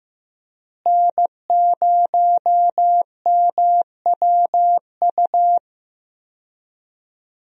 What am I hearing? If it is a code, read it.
Morse code, heard as a keyed tone, about 15 wpm, 700 Hz. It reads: N0MWU
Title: USA Call Signs - 5 Characters - Repeated 3x - 15wpm